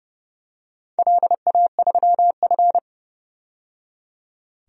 Morse code, heard as a keyed tone, about 30 wpm, 700 Hz. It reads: LA3F
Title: DXCC-100 Call Signs - 4 Characters - 30wpm